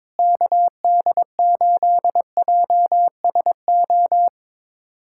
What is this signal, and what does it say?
Morse code, heard as a keyed tone, about 22 wpm, 700 Hz. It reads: KD8JSO
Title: USA Callsign DE Callsign 22wpm